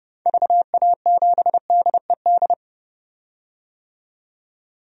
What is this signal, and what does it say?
Morse code, heard as a keyed tone, about 30 wpm, 700 Hz. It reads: VA7DED
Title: DXCC-100 Call Signs - 6 Characters - 30wpm